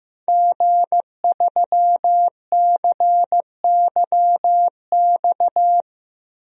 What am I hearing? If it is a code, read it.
Morse code, heard as a keyed tone, about 16 wpm, 700 Hz. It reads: G3CYX
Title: DXCC-100 Call Signs - 5 Characters - 15wpm